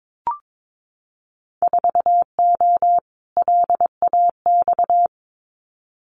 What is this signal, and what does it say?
Morse code, heard as a keyed tone, about 22 wpm, 700 Hz. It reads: 4O LAX
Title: Winter Field Day Exchanges Encore 22wpm